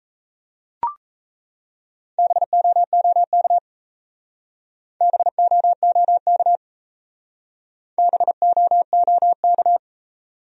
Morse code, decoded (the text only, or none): BOOK BOOK BOOK